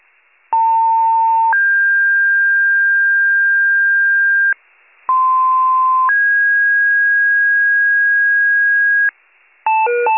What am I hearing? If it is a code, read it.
Two-tone page, heard as a one-second tone, then a three-second tone, about 900 then 1600 Hz, about 1040 then 1650 Hz. It is heard from 0.5 to 9.1 s.